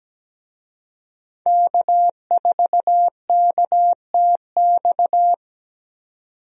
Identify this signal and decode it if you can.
Morse code, heard as a keyed tone, about 17 wpm, 700 Hz. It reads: K4KTX
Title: USA Call Signs - 5 Characters - 17wpm